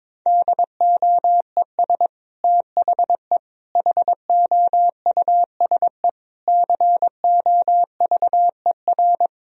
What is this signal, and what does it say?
Morse code, heard as a keyed tone, about 22 wpm, 700 Hz. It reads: DOES THE HOUSE COVER
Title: Sentences from Top 1000 Words - Review Part 2 - 22wpm